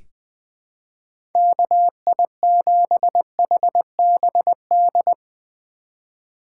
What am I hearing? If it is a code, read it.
Morse code, heard as a keyed tone, about 20 wpm, 700 Hz. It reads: KI7HBD
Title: USA Call Signs - 6 Characters - 4x Character Spacing - 20wpm